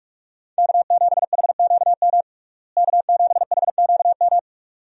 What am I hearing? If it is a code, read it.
Morse code, heard as a keyed tone, about 46 wpm, 700 Hz. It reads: K7HQM K7HQM
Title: USA Call Signs - 5 Characters - Repeated 2x - 45wpm